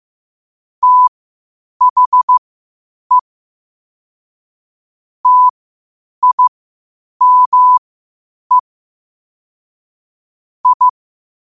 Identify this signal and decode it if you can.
Morse code, heard as a keyed tone, about 15 wpm, spaced as for 8 wpm, 1000 Hz. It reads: THE TIME I
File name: qotdfiles / 8wpm